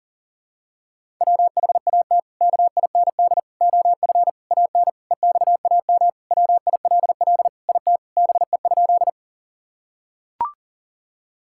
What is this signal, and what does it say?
Morse code, heard as a keyed tone, about 40 wpm, 700 Hz. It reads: WHAT KIND OF AN EXAM WILL IT BE?
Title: Sentences from Top 1000 Words - Review Part 5 - 40wpm